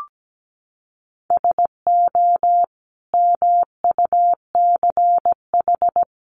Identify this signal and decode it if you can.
Morse code, heard as a keyed tone, about 17 wpm, 700 Hz. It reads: SO MUCH
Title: Sets of 5 Words 17wpm